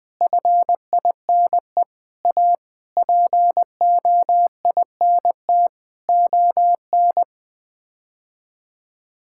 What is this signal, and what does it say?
Morse code, heard as a keyed tone, about 20 wpm, 700 Hz. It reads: FINE A POINT ON
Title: Sets of 8 Words - 20wpm